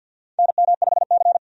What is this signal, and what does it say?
Morse code, heard as a keyed tone, about 52 wpm, 700 Hz. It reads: NM5C
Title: USA Call Signs - 4 Characters - 50wpm